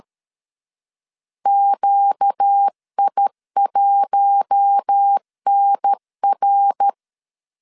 Morse code, heard as a keyed tone, about 13 wpm, 800 Hz. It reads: QI1NR
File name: LEZIONI / 6 PKX0 / 6 PKX 12wpm 2024